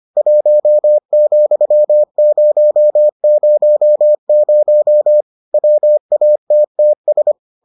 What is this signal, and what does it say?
Morse code, heard as a keyed tone, about 25 wpm, 600 Hz. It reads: 1,000 WATTS